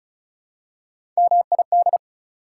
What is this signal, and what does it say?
Morse code, heard as a keyed tone, about 36 wpm, 690 Hz. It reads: MID
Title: Prefixes and Suffixes 35wpm